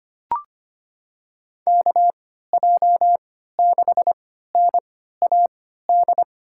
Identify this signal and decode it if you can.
Morse code, heard as a keyed tone, about 25 wpm, spaced as for 14 wpm, 700 Hz. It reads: KJ6NAD
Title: USA Call Signs - 6 Characters - 3x Character Spacing - 25wpm